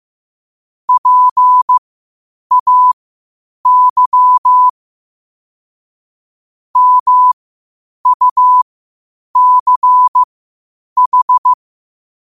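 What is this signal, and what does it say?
Morse code, heard as a keyed tone, about 15 wpm, spaced as for 8 wpm, 1000 Hz. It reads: PAY MUCH
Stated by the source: Morse code quotes at 8 words per minute from QOTD at 8 WPM.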